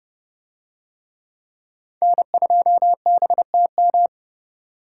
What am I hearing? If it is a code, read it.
Morse code, heard as a keyed tone, about 29 wpm, 700 Hz. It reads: N2BTM